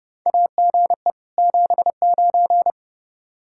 Morse code, heard as a keyed tone, about 30 wpm, 700 Hz. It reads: AGE 79
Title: K1USN SST QSO Practice 30wpm